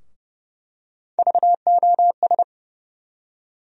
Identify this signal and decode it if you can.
Morse code, heard as a keyed tone, about 30 wpm, 700 Hz. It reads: VOS
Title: USA Call Sign Postfixes - 3 Letters 30wpm